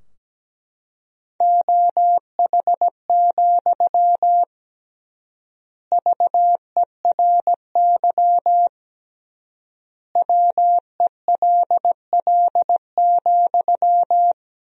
Morse code, heard as a keyed tone, about 17 wpm, 700 Hz. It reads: OH, VERY WELL,
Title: 3x Word-Spacing - Sentences from Top 500 Words - 17wpm